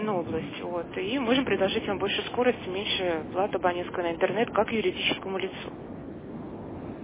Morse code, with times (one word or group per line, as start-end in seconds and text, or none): none